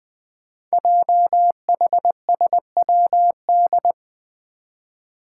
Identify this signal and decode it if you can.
Morse code, heard as a keyed tone, about 20 wpm, 700 Hz. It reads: JHSWD
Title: Code Groups - 5 Letters - 20wpm